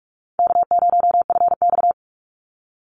Morse code, heard as a keyed tone, about 44 wpm, 700 Hz. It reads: K0FX
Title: Easy Call Signs - 45wpm